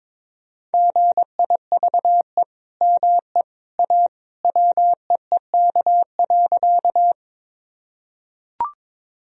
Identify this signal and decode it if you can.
Morse code, heard as a keyed tone, about 22 wpm, 700 Hz. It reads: GIVE ME A WEEK.
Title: Sentences from Top 1000 Words - Review Part 2 - 22wpm